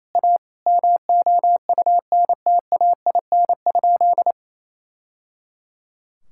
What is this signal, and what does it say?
Morse code, heard as a keyed tone, about 28 wpm, 700 Hz. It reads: A MOUNTAIN?
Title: Sentences from Top 1000 Words - Review Part 5 - 28wpm